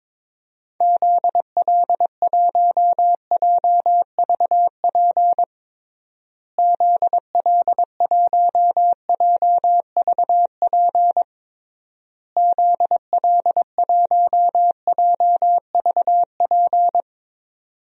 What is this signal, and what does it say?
Morse code, heard as a keyed tone, about 22 wpm, 700 Hz. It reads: ZL1JVP ZL1JVP ZL1JVP
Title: DXCC-100 Call Signs - 6 Characters - Repeated 3x - 22wpm